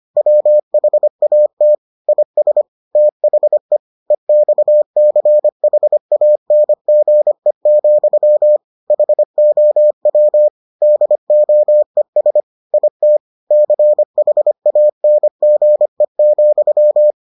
Morse code, heard as a keyed tone, about 25 wpm, 600 Hz. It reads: WHAT IS THE EXCHANGE, HOW DOES IT CHANGE,